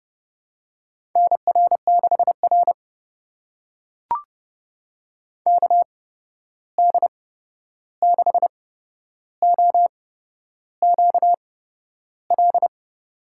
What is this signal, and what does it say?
Morse code, heard as a keyed tone, about 30 wpm, spaced as for 11 wpm, 700 Hz. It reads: NR6R KD6OQL
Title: USA Call Signs - 8x Character Spacing - 30wpm